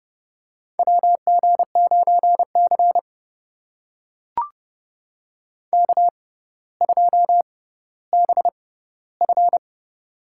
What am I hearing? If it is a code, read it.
Morse code, heard as a keyed tone, about 30 wpm, spaced as for 15 wpm, 700 Hz. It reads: WG9C K2BF
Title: USA Call Signs - 4 Characters - 6x Character Spacing - 30wpm